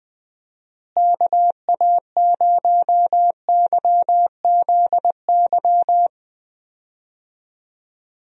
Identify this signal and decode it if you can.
Morse code, heard as a keyed tone, about 20 wpm, 700 Hz. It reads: KA0YZY
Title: USA Call Signs - 6 Characters - Repeated 3x - 20wpm